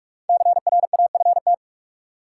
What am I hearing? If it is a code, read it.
Morse code, heard as a keyed tone, about 46 wpm, 700 Hz. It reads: CRAFT
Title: Sentences from Top 1000 Words - Review Part 6 - New Word Drill - 45wpm